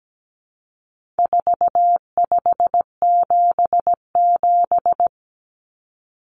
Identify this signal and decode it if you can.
Morse code, heard as a keyed tone, about 17 wpm, 700 Hz. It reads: 4577
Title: Common Cut Numbers - 4-Digit 17wpm